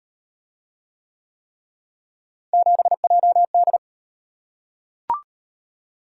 Morse code, heard as a keyed tone, about 37 wpm, 700 Hz. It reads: 7JD